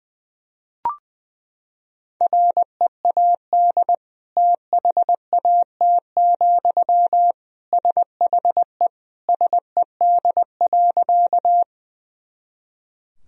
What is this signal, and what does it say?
Morse code, heard as a keyed tone, about 21 wpm, 700 Hz. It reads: READ THAT, SHE SED.
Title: Sentences from Top 200 Words and Abbreviations 20wpm